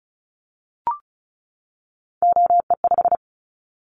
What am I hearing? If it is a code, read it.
Morse code, heard as a keyed tone, about 34 wpm, 700 Hz. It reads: OE5
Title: Callsign Prefixes for Top 50 DXCC - 35wpm